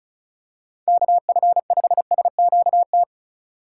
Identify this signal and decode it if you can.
Morse code, heard as a keyed tone, about 36 wpm, 700 Hz. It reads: KF5SQT
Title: USA Call Signs - 6 Characters - 2x Character Spacing - 35wpm